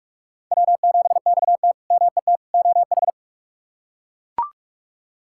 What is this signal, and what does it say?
Morse code, heard as a keyed tone, about 41 wpm, 700 Hz. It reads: W7XT MET OH
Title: K1USN SST QSO Practice - 45wpm